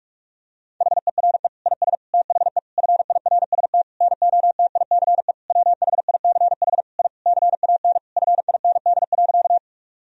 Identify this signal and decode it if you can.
Morse code, heard as a keyed tone, about 41 wpm, 700 Hz. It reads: HERE IS THE FIRST NOTICE WHICH I CAN FIND.
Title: Sentences from Top 1000 Words - Review Part 2 - 45wpm